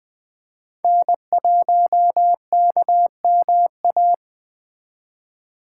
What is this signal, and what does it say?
Morse code, heard as a keyed tone, about 20 wpm, 700 Hz. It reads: N1KMA
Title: Speed Racing - USA Call Signs 17wpm